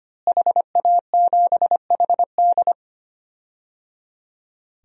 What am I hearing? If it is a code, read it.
Morse code, heard as a keyed tone, about 25 wpm, 700 Hz. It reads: HA7HD
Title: DXCC-100 Call Signs - 5 Characters - Repeated 3x - 25wpm